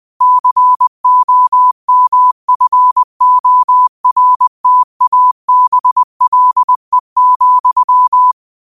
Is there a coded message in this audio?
Yes: Morse code from 0.2 to 8.3 s, heard as a keyed tone, about 20 wpm, 1000 Hz. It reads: COMFORTABLE,